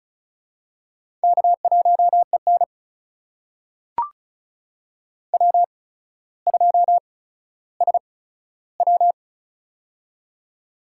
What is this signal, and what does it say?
Morse code, heard as a keyed tone, about 36 wpm, spaced as for 15 wpm, 700 Hz. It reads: K1EN W2SW
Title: USA Call Signs - 4 Characters - 8x Character Spacing - 35wpm